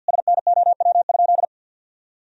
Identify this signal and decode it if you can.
Morse code, heard as a keyed tone, about 48 wpm, 700 Hz. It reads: SNOW?
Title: Sentences from Top 1000 Words - Review Part 3 - 50wpm